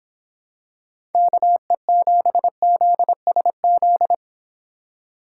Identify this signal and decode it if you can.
Morse code, heard as a keyed tone, about 26 wpm, 700 Hz. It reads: KE7ZSZ